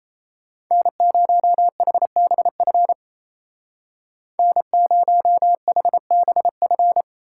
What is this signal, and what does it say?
Morse code, heard as a keyed tone, about 36 wpm, 700 Hz. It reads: N0HBF N0HBF